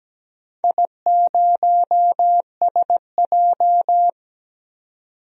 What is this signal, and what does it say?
Morse code, heard as a keyed tone, about 17 wpm, 700 Hz. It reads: I0SJ
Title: DXCC-100 Call Signs - 4 Characters - 17wpm